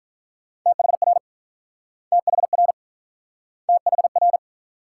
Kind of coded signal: Morse code

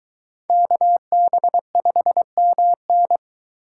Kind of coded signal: Morse code

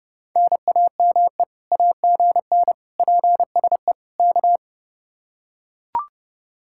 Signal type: Morse code

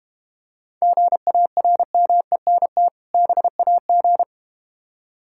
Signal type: Morse code

